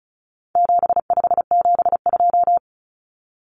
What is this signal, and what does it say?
Morse code, heard as a keyed tone, about 34 wpm, 700 Hz. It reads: 7572
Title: Numbers - 4-Digit 35wpm